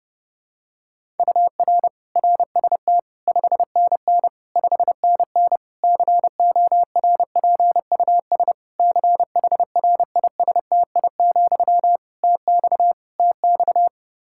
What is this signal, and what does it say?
Morse code, heard as a keyed tone, about 30 wpm, 700 Hz. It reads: UR RST 5NN 5NN CORPUS CHRISTI, TX TX
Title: K1USN SST QSO Practice 30wpm